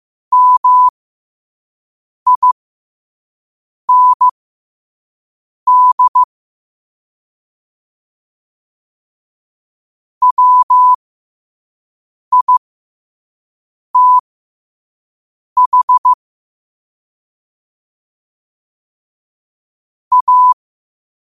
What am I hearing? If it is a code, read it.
Morse code, heard as a keyed tone, about 15 wpm, spaced as for 5 wpm, 1000 Hz. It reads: MIND WITH A
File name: qotdfiles / 5wpm